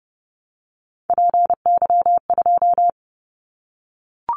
Morse code, heard as a keyed tone, about 30 wpm, 700 Hz. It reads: PY2
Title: Callsign Prefixes for Top 50 DXCC - 30wpm